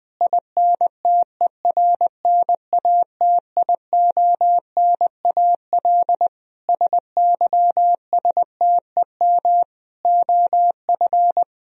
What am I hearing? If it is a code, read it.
Morse code, heard as a keyed tone, about 21 wpm, 700 Hz. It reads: INTERNATIONAL SYSTEM OF
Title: Sentences from Top 1000 Words - Review Part 4 - 20wpm